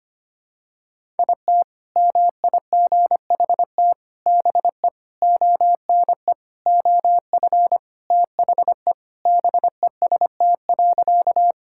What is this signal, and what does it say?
Morse code, heard as a keyed tone, about 25 wpm, 700 Hz. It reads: IT MIGHT BE ONE OF THE BEST.
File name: practice / 25wpm / Sentences with 4-Word Phrases - 25wpm